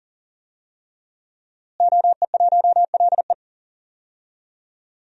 Morse code, heard as a keyed tone, about 40 wpm, 700 Hz. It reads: OE1LE
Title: DXCC-100 Call Signs - 5 Characters - Repeated 3x - 40wpm